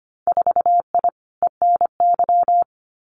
Morse code, heard as a keyed tone, about 25 wpm, 700 Hz. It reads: 4I ENY
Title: Winter Field Day Exchanges 25wpm